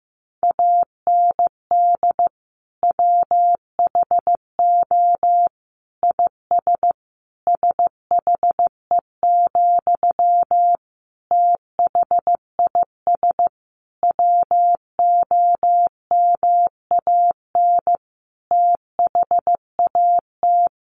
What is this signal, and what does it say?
Morse code, heard as a keyed tone, about 15 wpm, 700 Hz. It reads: AND WHO IS SHE, THIS WOMAN THAT
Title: Sentences from Top 1100 Words 15wpm